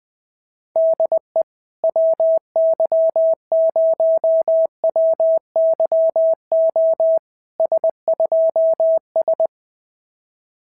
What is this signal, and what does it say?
Morse code, heard as a keyed tone, about 19 wpm, 650 Hz. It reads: DE WY0WYO S2S
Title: SOTA QSO Practice 20wpm